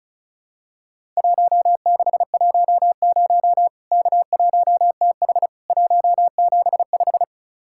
Morse code, heard as a keyed tone, about 36 wpm, 700 Hz. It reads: 1610 K1TH 175
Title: Worked All Europe DX Contest - QTC Exchange - 35wpm